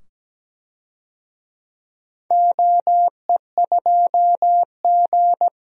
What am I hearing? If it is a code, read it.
Morse code, heard as a keyed tone, about 17 wpm, 700 Hz. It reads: OE2G
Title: DXCC-100 Call Signs - 4 Characters - 17wpm